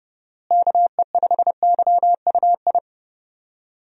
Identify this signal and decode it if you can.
Morse code, heard as a keyed tone, about 30 wpm, 700 Hz. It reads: KE5YUI
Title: USA Call Signs - 6 Characters - Repeated 2x - 30wpm